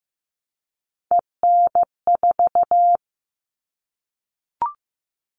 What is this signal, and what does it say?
Morse code, heard as a keyed tone, about 15 wpm, 700 Hz. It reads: EN4
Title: All Cut Numbers - 3-Digit 15wpm